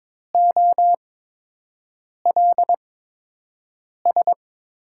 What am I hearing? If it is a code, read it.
Morse code, heard as a keyed tone, about 22 wpm, spaced as for 6 wpm, 700 Hz. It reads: OLS
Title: USA Call Signs - 5 Characters - 8x Character Spacing - 22wpm